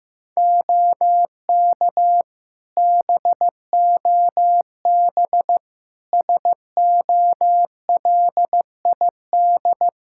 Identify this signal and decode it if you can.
Morse code, heard as a keyed tone, about 15 wpm, 700 Hz. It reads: OK BOB SOLID
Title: Practice QSO - Exchange 3 - More Info - 15wpm